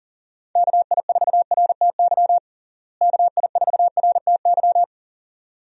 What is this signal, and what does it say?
Morse code, heard as a keyed tone, about 40 wpm, 700 Hz. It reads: KI4RTY KI4RTY
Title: USA Call Signs - 6 Characters - Repeated 2x - 40wpm